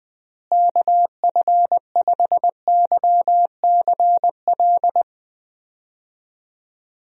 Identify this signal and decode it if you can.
Morse code, heard as a keyed tone, about 20 wpm, 700 Hz. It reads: KF5YCL
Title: USA Call Signs - 6 Characters - Repeated 3x - 20wpm